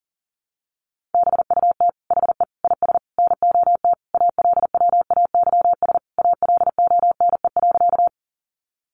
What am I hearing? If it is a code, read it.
Morse code, heard as a keyed tone, about 40 wpm, 700 Hz. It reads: BUT HE IS NOT ALWAYS ALONE.
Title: Sentences from Top 1000 Words 40wpm